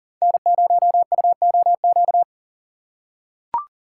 Morse code, heard as a keyed tone, about 37 wpm, 700 Hz. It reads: N0UOQ
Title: USA Call Signs - 5 Characters - 4x Character Spacing - 40wpm